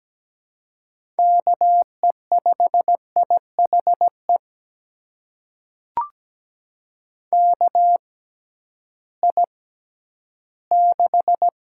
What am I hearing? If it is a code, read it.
Morse code, heard as a keyed tone, about 17 wpm, spaced as for 11 wpm, 700 Hz. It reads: KE5IHE KI6
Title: USA Call Signs - 6 Characters - 6x Character Spacing - 17wpm